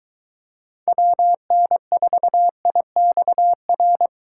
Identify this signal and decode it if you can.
Morse code, heard as a keyed tone, about 23 wpm, 700 Hz. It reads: WN4IXR